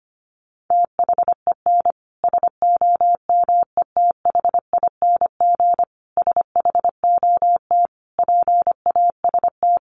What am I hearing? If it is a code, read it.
Morse code, heard as a keyed tone, about 25 wpm, 700 Hz. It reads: THEN SOMETHING SHOT PAST
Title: Sentences from Top 1100 Words 25wpm